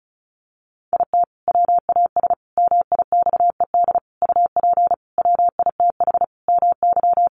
Morse code, heard as a keyed tone, about 34 wpm, 700 Hz. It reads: IT WAS MIXED UP WITH MY
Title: Sentences from Top 1500 Words 35wpm